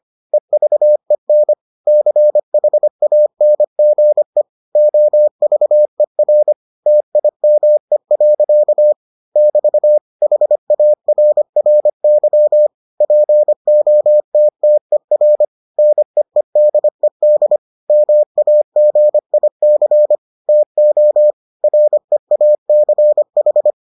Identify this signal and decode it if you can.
Morse code, heard as a keyed tone, about 25 wpm, 600 Hz. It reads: EVEN CHANGE OVER TIME. <BT> HARRY POTTER NEEDED MAGIC TO REACH